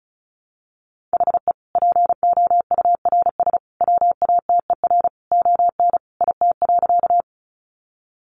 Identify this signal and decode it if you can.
Morse code, heard as a keyed tone, about 34 wpm, 700 Hz. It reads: HE POURS WATER ON IT.
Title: Additional Easy Phrases 35wpm